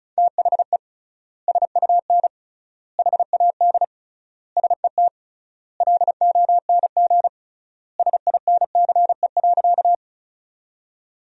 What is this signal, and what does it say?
Morse code, heard as a keyed tone, about 36 wpm, 700 Hz. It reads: THE SUN HAD SET LONG SINCE.
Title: 3x Word-Spacing - Sentences from Top 300 Words - 35wpm